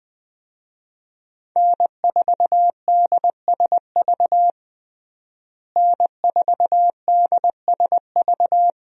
Morse code, heard as a keyed tone, about 20 wpm, 700 Hz. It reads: N4DSV N4DSV
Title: USA Call Signs - 5 Characters - Repeated 2x - 20wpm